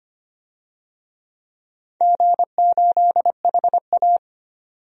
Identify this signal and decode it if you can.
Morse code, heard as a keyed tone, about 25 wpm, 690 Hz. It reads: G8HA
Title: DXCC-100 Call Signs - 4 Characters - 25wpm